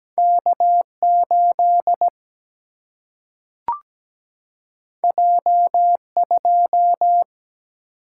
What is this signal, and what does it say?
Morse code, heard as a keyed tone, about 17 wpm, 700 Hz. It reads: K8 J2
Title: Single-Letter Single-Number 17wpm